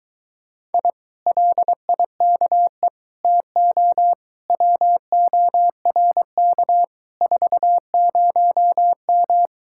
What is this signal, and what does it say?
Morse code, heard as a keyed tone, about 22 wpm, 700 Hz. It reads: I LIKE TO WORK 40M